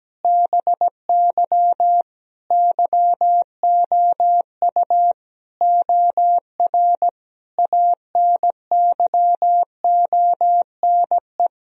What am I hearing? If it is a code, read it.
Morse code, heard as a keyed tone, about 17 wpm, 700 Hz. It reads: BY YOU OR ANYONE
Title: Sentences from Top 1000 Words - Review Part 1 - 17wpm